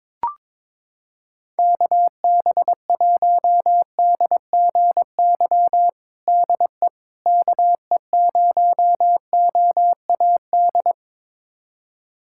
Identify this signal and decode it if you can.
Morse code, heard as a keyed tone, about 22 wpm, 700 Hz. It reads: KB1DGY DE KE0OAD
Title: USA Callsign DE Callsign - 6 Characters - 22wpm